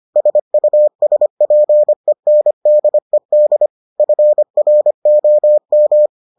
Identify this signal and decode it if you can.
Morse code, heard as a keyed tone, about 25 wpm, 600 Hz. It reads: SUSPENDED FROM